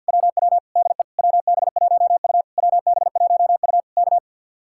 Morse code, heard as a keyed tone, about 64 wpm, 700 Hz. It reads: WW DE WB1U WB1U K